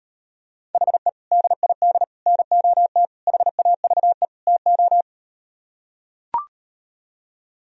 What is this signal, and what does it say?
Morse code, heard as a keyed tone, about 38 wpm, 700 Hz. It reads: HE DID NOT HAVE TO